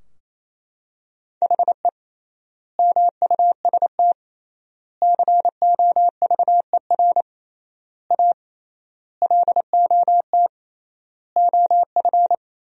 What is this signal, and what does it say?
Morse code, heard as a keyed tone, about 28 wpm, 700 Hz. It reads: HE MUST COVER A LOT OF
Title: 3x Word-Spacing - Sentences from Top 500 Words - 28wpm